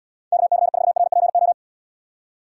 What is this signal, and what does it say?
Morse code, heard as a keyed tone, about 78 wpm, 700 Hz. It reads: KC4RPO